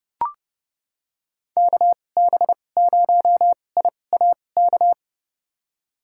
Morse code, heard as a keyed tone, about 30 wpm, spaced as for 22 wpm, 700 Hz. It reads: KB0IAK
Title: USA Call Signs - 6 Characters - 2x Character Spacing - 30wpm